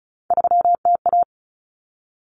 Morse code, heard as a keyed tone, about 34 wpm, 700 Hz. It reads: 3TA
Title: Common Cut Numbers - 3-Digit 35wpm